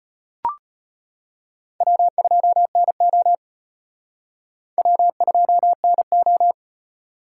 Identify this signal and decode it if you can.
Morse code, heard as a keyed tone, about 31 wpm, 700 Hz. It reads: W2NO W2NO
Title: Speed Racing - USA Call Signs 25wpm